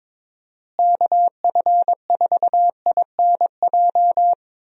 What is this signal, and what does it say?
Morse code, heard as a keyed tone, about 22 wpm, 700 Hz. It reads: KF4INJ
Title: USA Call Signs - 6 Characters - Repeated 3x - 22wpm